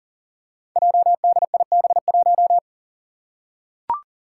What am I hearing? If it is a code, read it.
Morse code, heard as a keyed tone, about 40 wpm, 700 Hz. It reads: JDIB1
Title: Five Challenging Letter-Number Combinations 40wpm